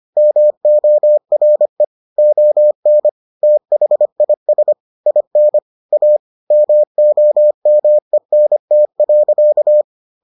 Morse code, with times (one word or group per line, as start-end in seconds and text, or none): 0.2-1.8 MORE
2.2-3.1 ON
3.4-4.7 THIS
5.1-5.6 IN
5.9-6.2 A
6.5-9.8 MOMENT.